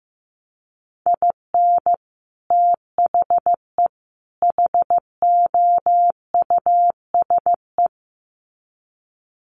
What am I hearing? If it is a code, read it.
Morse code, heard as a keyed tone, about 15 wpm, 700 Hz. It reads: IN THE HOUSE
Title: Sets of 3 Words 15wpm